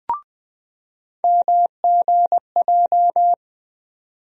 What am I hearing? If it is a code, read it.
Morse code, heard as a keyed tone, about 20 wpm, 700 Hz. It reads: MGJ
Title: USA Call Sign Postfixes - 3 Letters 20wpm